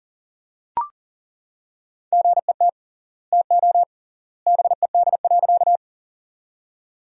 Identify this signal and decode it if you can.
Morse code, heard as a keyed tone, about 37 wpm, 700 Hz. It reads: GET TO BED.
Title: 3x Word-Spacing - Sentences from Top 500 Words - 40wpm